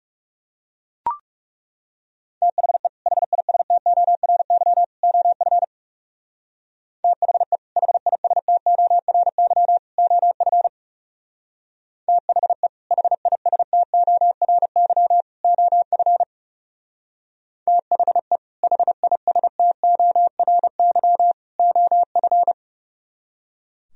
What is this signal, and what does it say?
Morse code, heard as a keyed tone, about 38 wpm, 700 Hz. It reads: THE HISTORY OF THE HISTORY OF THE HISTORY OF THE HISTORY OF